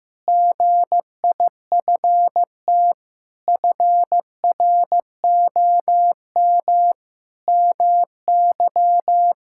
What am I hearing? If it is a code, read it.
Morse code, heard as a keyed tone, about 14 wpm, 700 Hz. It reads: GIFT FROM MY
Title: Sentences from Top 1000 Words - Review Part 6 - 15wpm